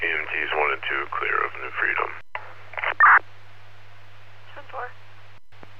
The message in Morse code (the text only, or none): none